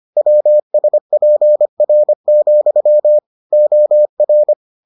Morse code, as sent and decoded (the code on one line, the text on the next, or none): .-- ... .--. .-. --..-- / --- .-.
WSPR, OR